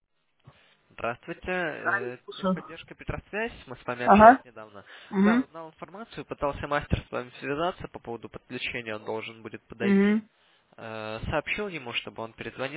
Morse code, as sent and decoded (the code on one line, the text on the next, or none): none